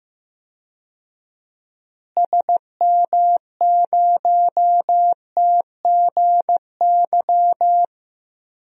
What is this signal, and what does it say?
Morse code, heard as a keyed tone, about 16 wpm, 700 Hz. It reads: SM0TGY
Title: DXCC-100 Call Signs - 6 Characters - Repeated 2x - 15wpm